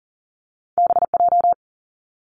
Morse code, heard as a keyed tone, about 42 wpm, 700 Hz. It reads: BJ
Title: Top Two Letter Combinations in Words 40wpm